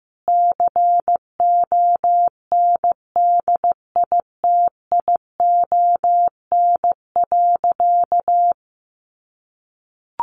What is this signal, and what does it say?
Morse code, heard as a keyed tone, about 15 wpm, 700 Hz. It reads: CONDITION.
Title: Sentences from Top 1100 Words 15wpm